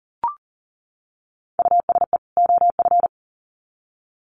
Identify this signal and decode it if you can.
Morse code, heard as a keyed tone, about 40 wpm, 700 Hz. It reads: USE OF
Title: Sets of 2 Words 40wpm